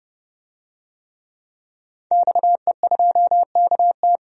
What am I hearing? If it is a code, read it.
Morse code, heard as a keyed tone, about 32 wpm, 700 Hz. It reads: XE2KT
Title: DXCC-100 Call Signs - 5 Characters - 30wpm